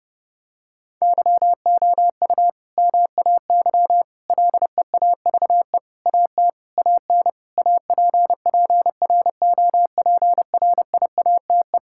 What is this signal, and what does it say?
Morse code, heard as a keyed tone, about 30 wpm, 700 Hz. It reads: YOU MAY LEAVE AT AN APPROPRIATE
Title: Sentences from Top 1000 Words - Review Part 5 - 30wpm